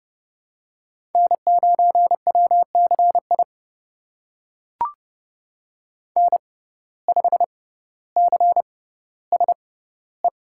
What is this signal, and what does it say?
Morse code, heard as a keyed tone, about 30 wpm, spaced as for 15 wpm, 700 Hz. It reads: N9WCI N5CSE
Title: USA Call Signs - 5 Characters - 6x Character Spacing - 30wpm